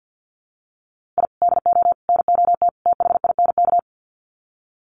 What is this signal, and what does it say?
Morse code, heard as a keyed tone, about 50 wpm, 700 Hz. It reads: I DO NOT THINK
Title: Sets of 4 Words 50wpm